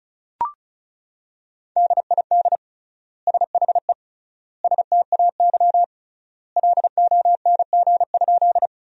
Morse code, heard as a keyed tone, about 34 wpm, 700 Hz. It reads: DID SHE STAY LONG?
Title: 3x Word-Spacing - Sentences from Top 500 Words - 35wpm